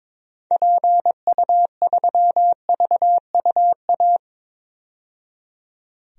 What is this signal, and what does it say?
Morse code, heard as a keyed tone, about 22 wpm, 700 Hz. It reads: PU3VUA
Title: DXCC-100 Call Signs - 6 Characters - 22wpm